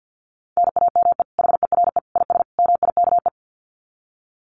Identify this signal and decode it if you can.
Morse code, heard as a keyed tone, about 50 wpm, 700 Hz. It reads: NAME HERE IS MIKE
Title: Common QSO Phrases 50wpm